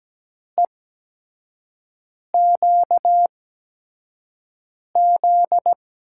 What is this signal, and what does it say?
Morse code, heard as a keyed tone, about 17 wpm, spaced as for 5 wpm, 700 Hz. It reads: EQZ
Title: USA Call Signs - 6 Characters - 8x Character Spacing - 17wpm